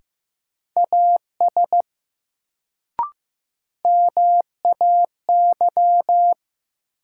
Morse code, heard as a keyed tone, about 15 wpm, 700 Hz. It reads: AS MAY
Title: Sentences from Top 100 Words - New Word Drill - 15wpm